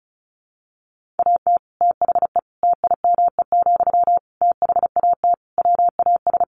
Morse code, heard as a keyed tone, about 34 wpm, 700 Hz. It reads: AT THE TIME, THAT WAS
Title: Sentences from Top 1400 Words 35wpm